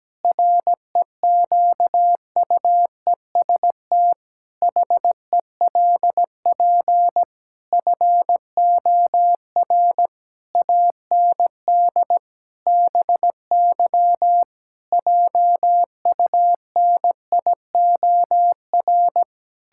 Morse code, heard as a keyed tone, about 17 wpm, 700 Hz. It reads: REQUEST HELP FOR AND BY JUNIOR
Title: Pangram Challenge v2 - 17wpm